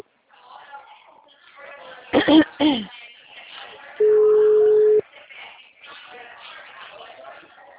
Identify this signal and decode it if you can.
Ringback tone (Europe, 425 Hz), with speech.
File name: out- -2013- - - .70131